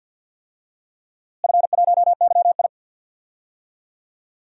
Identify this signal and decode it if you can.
Morse code, heard as a keyed tone, about 46 wpm, 700 Hz. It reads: F1YI
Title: DXCC-100 Call Signs - 4 Characters - Repeated 3x - 50wpm